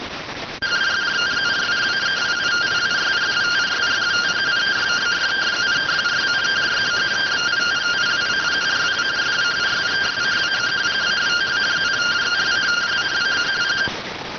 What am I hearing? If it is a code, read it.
Two-tone teleprinter signal, about 170 Hz shift.